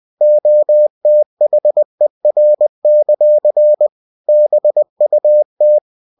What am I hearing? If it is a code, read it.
Morse code, heard as a keyed tone, about 20 wpm, 600 Hz. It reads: OTHER; BUT
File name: cw-books / winnie-the-pooh / winnie-20wpm-page-0034